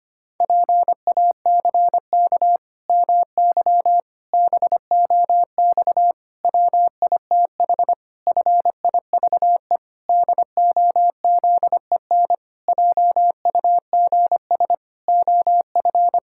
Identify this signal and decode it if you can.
Morse code, heard as a keyed tone, about 25 wpm, 700 Hz. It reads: PACK MY BOX WITH FIVE DOZEN JUGS OF
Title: Pangram Challenge v2 - 25wpm